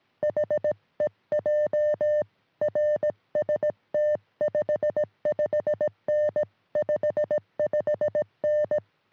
Morse code, heard as a keyed tone, about 18 wpm, 600 Hz. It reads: HEJ RST 55N 55N
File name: sm7lcb / 1943 ra3le 23cm